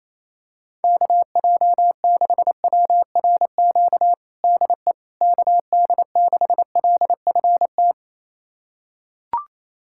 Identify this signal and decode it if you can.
Morse code, heard as a keyed tone, about 28 wpm, 700 Hz. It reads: KJ6WRQ DE KD6LFT
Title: USA Callsign DE Callsign - 6 Characters - 28wpm